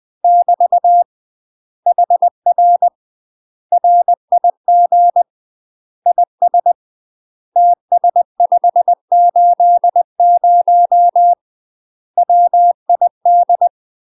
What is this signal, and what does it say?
Morse code, heard as a keyed tone, about 20 wpm, 700 Hz. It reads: <BT> HR RIG IS TS580 WID
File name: UK MASTER QSOs TRAINING / 20WPM / UK MASTER QSOs 0073